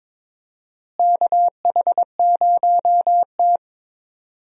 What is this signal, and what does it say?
Morse code, heard as a keyed tone, about 22 wpm, 700 Hz. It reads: KH0T
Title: USA Call Signs - 4 Characters - Repeated 3x - 22wpm